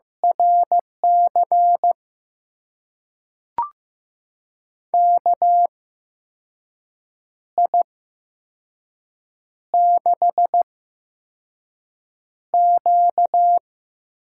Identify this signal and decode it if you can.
Morse code, heard as a keyed tone, about 15 wpm, spaced as for 5 wpm, 700 Hz. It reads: RC KI6Q
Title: USA Call Signs - 6 Characters - 8x Character Spacing - 15wpm